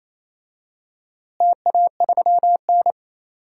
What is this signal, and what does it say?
Morse code, heard as a keyed tone, about 27 wpm, 700 Hz. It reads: TA3N
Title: DXCC-100 Call Signs - 4 Characters - Repeated 2x - 28wpm